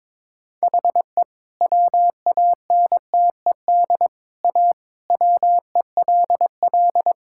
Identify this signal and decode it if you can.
Morse code, heard as a keyed tone, about 21 wpm, 700 Hz. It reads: HE WANTED A WELL
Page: morse-code-ninja-practice-22wpm